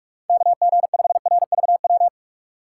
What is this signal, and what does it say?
Morse code, heard as a keyed tone, about 43 wpm, 700 Hz. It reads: KG5RVW
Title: USA Call Signs - 6 Characters - 2x Character Spacing - 45wpm